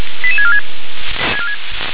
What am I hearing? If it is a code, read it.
Five-tone selective call (ZVEI), heard as quick runs of five tones, tones of about 70 ms. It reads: 88646
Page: ZVEI